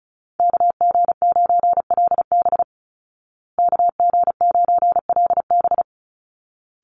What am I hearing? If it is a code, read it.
Morse code, heard as a keyed tone, about 34 wpm, 700 Hz. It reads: KG9LB KG9LB
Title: Call Signs Repeated 3x - 35wpm